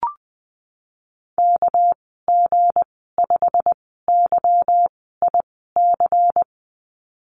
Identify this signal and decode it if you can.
Morse code, heard as a keyed tone, about 20 wpm, spaced as for 14 wpm, 700 Hz. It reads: KG5YIC
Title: 200 Lesson 32g - Letter k - Call Signs 20wpm